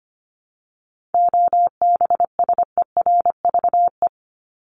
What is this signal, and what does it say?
Morse code, heard as a keyed tone, about 25 wpm, 700 Hz. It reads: OBSERVE
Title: ED Suffix - Top 100 Words with Base Words 25wpm